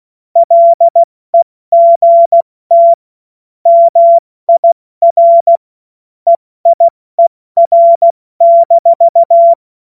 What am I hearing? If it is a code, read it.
Morse code, heard as a keyed tone, about 16 wpm, 680 Hz. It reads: LEGT MIR EIER-
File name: Morsekurs / Witze / Witz 01. Februar 2022 70 F80